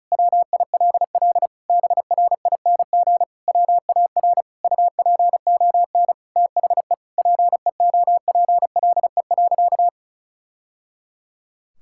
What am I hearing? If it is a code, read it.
Morse code, heard as a keyed tone, about 34 wpm, 700 Hz. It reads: WILL BRING WAR UPON THE PEOPLE.
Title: Sentences from Top 1000 Words - Review Part 6 - 35wpm